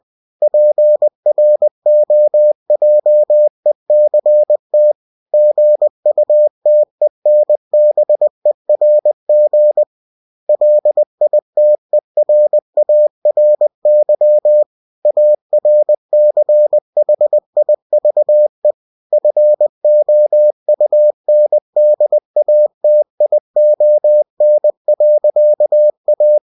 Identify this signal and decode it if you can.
Morse code, heard as a keyed tone, about 20 wpm, 600 Hz. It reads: PROJECT GUTENBERG LITERARY ARCHIVE FOUNDATION.A
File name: cw-books / winnie-the-pooh / winnie-20wpm-page-0117